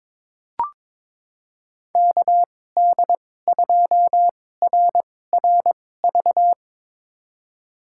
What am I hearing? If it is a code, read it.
Morse code, heard as a keyed tone, about 22 wpm, spaced as for 16 wpm, 700 Hz. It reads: KD2RRV
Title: USA Call Signs - 6 Characters - 2x Character Spacing - 22wpm